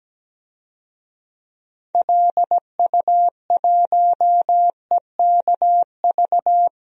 Morse code, heard as a keyed tone, about 17 wpm, 700 Hz. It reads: LU1EKV
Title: DXCC-100 Call Signs - 6 Characters - Repeated 3x - 17wpm